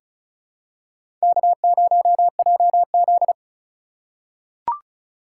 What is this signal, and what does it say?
Morse code, heard as a keyed tone, about 36 wpm, 700 Hz. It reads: K0JZ
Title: USA Call Signs - 4 Characters - 3x Character Spacing - 35wpm